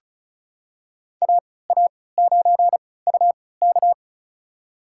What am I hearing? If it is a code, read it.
Morse code, heard as a keyed tone, about 36 wpm, spaced as for 20 wpm, 700 Hz. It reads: AA9UK
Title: USA Call Signs - 5 Characters - 3x Character Spacing - 35wpm